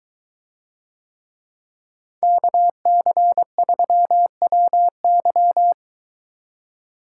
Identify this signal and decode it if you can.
Morse code, heard as a keyed tone, about 22 wpm, 700 Hz. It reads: KC3WY